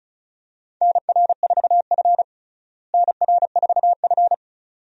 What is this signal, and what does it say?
Morse code, heard as a keyed tone, about 34 wpm, 700 Hz. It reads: NR4F NR4F
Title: USA Call Signs - 4 Characters - Repeated 2x - 35wpm